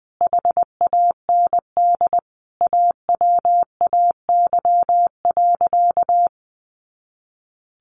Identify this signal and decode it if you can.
Morse code, heard as a keyed tone, about 19 wpm, 700 Hz. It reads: HAND AWAY.
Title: A Sound Of Thunder 20wpm